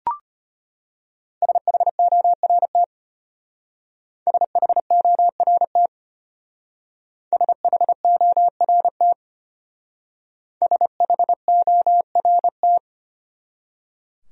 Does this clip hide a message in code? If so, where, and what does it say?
Yes: Morse code from 1.4 to 12.8 s, heard as a keyed tone, about 33 wpm, 700 Hz. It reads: SHORT SHORT SHORT SHORT